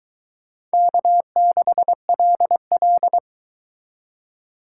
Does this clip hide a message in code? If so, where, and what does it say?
Yes: Morse code from 0.7 to 3.2 s, heard as a keyed tone, about 22 wpm, 700 Hz. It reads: K6LL